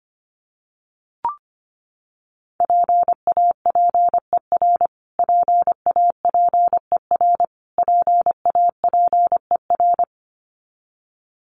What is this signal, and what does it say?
Morse code, heard as a keyed tone, about 25 wpm, 700 Hz. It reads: PAPER PAPER PAPER
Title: Top 300 Words Repeated 25wpm